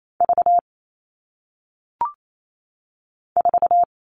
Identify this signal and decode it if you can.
Morse code, heard as a keyed tone, about 28 wpm, 700 Hz. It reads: V 4
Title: All Cut Numbers - 1-Digit 28wpm